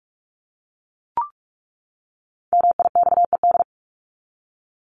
Morse code, heard as a keyed tone, about 44 wpm, 700 Hz. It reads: MIXED
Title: Top 1251-1500 Words 45wpm